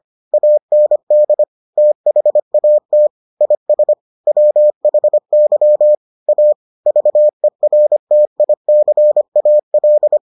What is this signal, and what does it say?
Morse code, heard as a keyed tone, about 25 wpm, 600 Hz. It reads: AND THAT IS WHY A VERTICAL